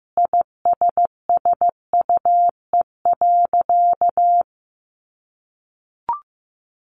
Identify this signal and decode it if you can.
Morse code, heard as a keyed tone, about 15 wpm, 700 Hz. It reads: ISSUE.
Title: Sentences from Top 900 Words 15wpm